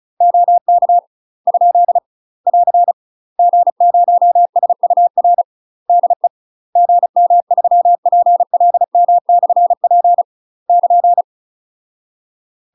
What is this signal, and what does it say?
Morse code, heard as a keyed tone, about 36 wpm, 700 Hz. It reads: OK ? <AR> G0SUR DE GM3PLM/P <KN>
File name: UK MASTER QSOs TRAINING / 35WPM / UK MASTER QSOs 0044